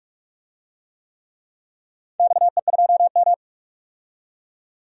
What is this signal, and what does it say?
Morse code, heard as a keyed tone, about 46 wpm, 700 Hz. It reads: XE2M
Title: DXCC-100 Call Signs - 4 Characters - 45wpm